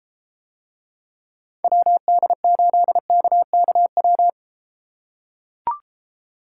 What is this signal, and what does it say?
Morse code, heard as a keyed tone, about 33 wpm, 700 Hz. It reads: WD8KKW